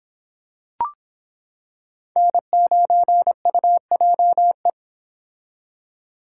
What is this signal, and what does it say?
Morse code, heard as a keyed tone, about 24 wpm, 700 Hz. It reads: N9UJE